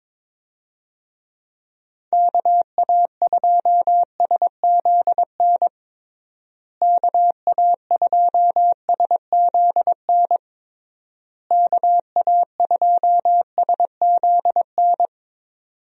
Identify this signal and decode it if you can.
Morse code, heard as a keyed tone, about 23 wpm, 700 Hz. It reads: KA2SZN KA2SZN KA2SZN
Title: USA Call Signs - 6 Characters - Repeated 3x - 22wpm